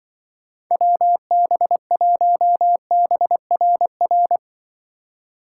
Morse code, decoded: WB1BRR

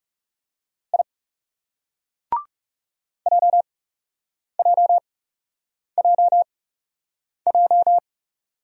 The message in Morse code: I JJJJ

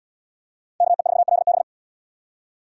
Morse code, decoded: KE4LB

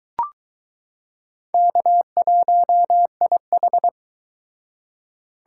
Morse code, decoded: K1IH

